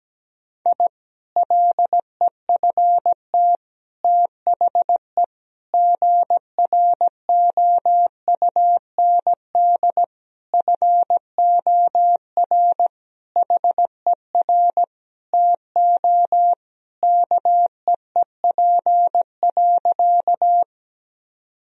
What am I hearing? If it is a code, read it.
Morse code, heard as a keyed tone, about 17 wpm, 700 Hz. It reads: I LEFT THE GROUND FOR HER TO KEEP.
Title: Sentences from Top 1000 Words - Review Part 2 - 17wpm